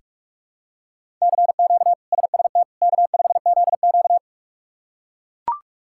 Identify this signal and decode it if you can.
Morse code, heard as a keyed tone, about 42 wpm, 700 Hz. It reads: CQ SST K5ZQ
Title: K1USN SST QSO Practice - 45wpm